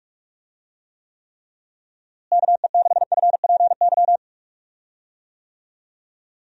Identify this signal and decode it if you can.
Morse code, heard as a keyed tone, about 46 wpm, 700 Hz. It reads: KE6FPY
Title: USA Call Signs - 6 Characters - Repeated 3x - 45wpm